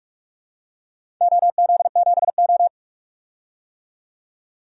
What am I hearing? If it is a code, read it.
Morse code, heard as a keyed tone, about 47 wpm, 700 Hz. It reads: OZ7O
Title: DXCC-100 Call Signs - 4 Characters - 45wpm